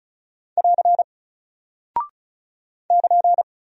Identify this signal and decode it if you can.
Morse code, heard as a keyed tone, about 36 wpm, 700 Hz. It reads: <AR> <KN>
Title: Single Character (Letter:Number:Punctuation:Prosign) Alt 35wpm